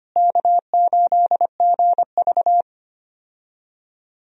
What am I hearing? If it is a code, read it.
Morse code, heard as a keyed tone, about 25 wpm, 700 Hz. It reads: K8GV
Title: USA Call Signs - 4 Characters - Repeated 2x - 25wpm